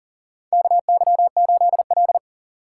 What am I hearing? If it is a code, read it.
Morse code, heard as a keyed tone, about 40 wpm, 700 Hz. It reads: KY8L
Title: USA Call Signs - 4 Characters - 4x Character Spacing - 40wpm